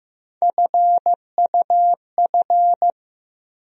Morse code, heard as a keyed tone, about 15 wpm, 700 Hz. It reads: FUF
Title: USA Call Sign Postfixes - 3 Letters 15wpm